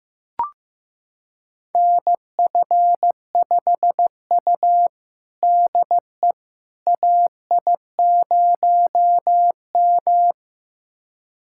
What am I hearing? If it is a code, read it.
Morse code, heard as a keyed tone, about 15 wpm, 700 Hz. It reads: NF5U DE AI0M
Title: USA Callsign DE Callsign - 4 Characters - 15wpm